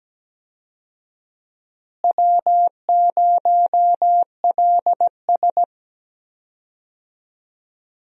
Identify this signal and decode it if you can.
Morse code, heard as a keyed tone, about 17 wpm, 700 Hz. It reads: W0LS
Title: USA Call Signs - 4 Characters - Repeated 2x - 17wpm